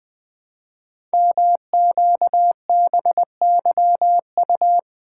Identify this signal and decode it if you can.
Morse code, heard as a keyed tone, about 20 wpm, 700 Hz. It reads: MQBYU
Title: Code Groups - 5 Letters - 20wpm